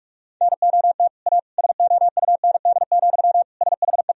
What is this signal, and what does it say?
Morse code, heard as a keyed tone, about 48 wpm, 700 Hz. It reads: NOT A SOUND, SHE
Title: Sentences from Top 1000 Words - Review Part 1 - 45wpm